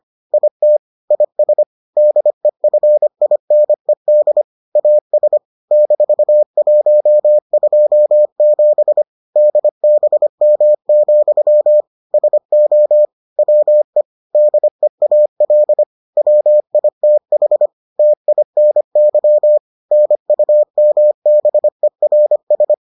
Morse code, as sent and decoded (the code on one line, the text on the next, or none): .. - / .. ... / -.. ..-. .. -. -.. / .- ... / -....- .---- ..--- --... / -.. -... -- --..-- / ... --- / .-- . / -.. .- .-.. / .-- .. - .... / - .. -. -.-- / -. ..- -- -... .-. ...
IT IS DEFINED AS -127 DBM, SO WE DEAL WITH TINY NUMBERS